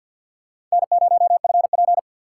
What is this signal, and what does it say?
Morse code, heard as a keyed tone, about 52 wpm, 700 Hz. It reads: N0FP
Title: USA Call Signs - 4 Characters - 3x Character Spacing - 50wpm